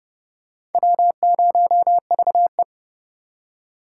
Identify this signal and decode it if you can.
Morse code, heard as a keyed tone, about 28 wpm, 700 Hz. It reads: W0VE